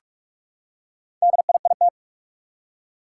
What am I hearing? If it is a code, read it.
Morse code, heard as a keyed tone, about 44 wpm, 700 Hz. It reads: DIIT
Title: Four Challenging Letter-Number Combinations 45wpm